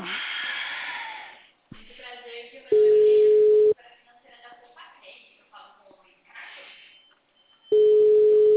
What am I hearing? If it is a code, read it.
Ringback tone (Europe, 425 Hz), with speech.
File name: out- -2003- - - .72382